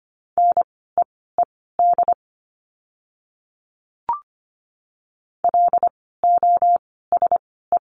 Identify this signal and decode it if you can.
Morse code, heard as a keyed tone, about 25 wpm, spaced as for 16 wpm, 700 Hz. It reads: NEED LOSE
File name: course / 25:10wpm / 094 Lesson 16f - Letter c - Words 25wpm